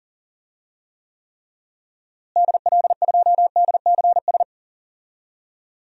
Morse code, heard as a keyed tone, about 40 wpm, 700 Hz. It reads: DL2DCS
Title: DXCC-100 Call Signs - 6 Characters - Repeated 3x - 40wpm